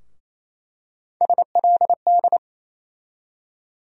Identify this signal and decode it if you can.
Morse code, heard as a keyed tone, about 28 wpm, 700 Hz. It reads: SLD
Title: USA Call Sign Postfixes - 3 Letters 28wpm